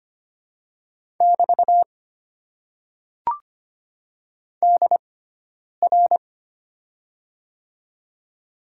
Morse code, heard as a keyed tone, about 25 wpm, spaced as for 9 wpm, 690 Hz. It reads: <BT> DR
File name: Top 90 QSO Elements - 6x Character Spacing - 25wpm